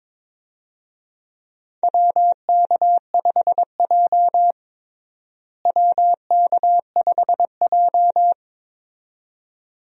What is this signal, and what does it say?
Morse code, heard as a keyed tone, about 21 wpm, 700 Hz. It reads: WK5J WK5J
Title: USA Call Signs - 4 Characters - Repeated 2x - 22wpm